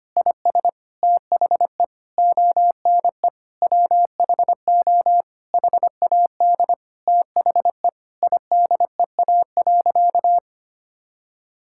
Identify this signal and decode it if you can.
Morse code, heard as a keyed tone, about 25 wpm, 700 Hz. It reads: IS THE ONE WHO HAD THE IDEA.
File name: practice / 25wpm / Sentences with 2-Word Phrases - 25wpm